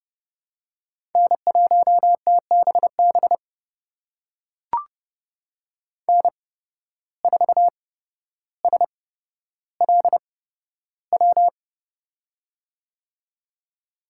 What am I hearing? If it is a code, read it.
Morse code, heard as a keyed tone, about 30 wpm, spaced as for 13 wpm, 700 Hz. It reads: N1TBB N4SLW
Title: USA Call Signs - 5 Characters - 8x Character Spacing - 30wpm